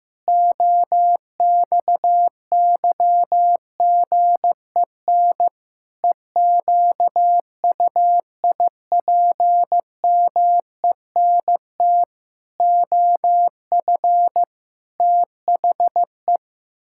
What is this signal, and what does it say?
Morse code, heard as a keyed tone, about 16 wpm, 700 Hz. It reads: OXYGEN EQUIPMENT OF THE
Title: Pangram Challenge v2 - 15wpm